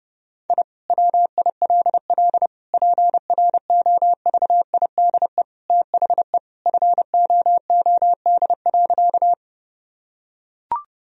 Morse code, heard as a keyed tone, about 32 wpm, 700 Hz. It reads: I WILL PROVIDE THE FOOD.
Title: Sentences from Top 1000 Words - Review Part 3 - 30wpm